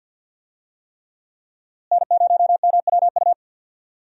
Morse code, heard as a keyed tone, about 53 wpm, 700 Hz. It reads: N0MWU
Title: USA Call Signs - 5 Characters - Repeated 3x - 50wpm